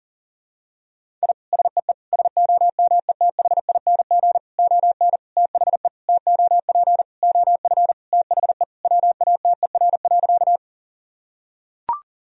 Morse code, heard as a keyed tone, about 40 wpm, 700 Hz. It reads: I SEE SOMETHING ON THE TOP OF THE WATER.
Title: Sentences from Top 1000 Words - Review Part 2 - 40wpm